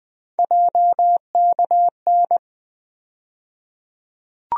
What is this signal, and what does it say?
Morse code, heard as a keyed tone, about 20 wpm, 700 Hz. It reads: JKN
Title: USA Call Sign Postfixes - 3 Letters 20wpm